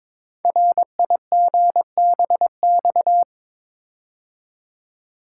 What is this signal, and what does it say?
Morse code, heard as a keyed tone, about 22 wpm, 700 Hz. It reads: RIGBX
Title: Code Groups - 5 Letters - 22wpm